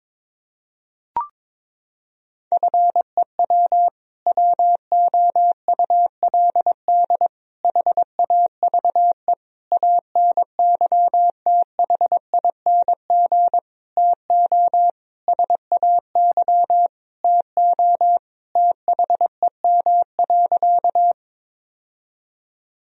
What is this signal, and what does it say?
Morse code, heard as a keyed tone, about 22 wpm, 700 Hz. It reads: FEW WOULD HAVE ANYTHING TO SAY TO THEM.
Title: Sentences from Top 1000 Words - Review Part 1 - 22wpm